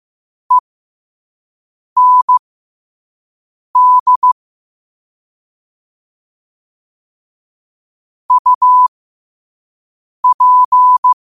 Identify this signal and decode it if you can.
Morse code, heard as a keyed tone, about 15 wpm, spaced as for 5 wpm, 1000 Hz. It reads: END UP